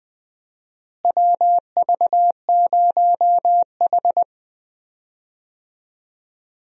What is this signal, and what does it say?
Morse code, heard as a keyed tone, about 20 wpm, 700 Hz. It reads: WV0H
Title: USA Call Signs - 4 Characters - Repeated 3x - 20wpm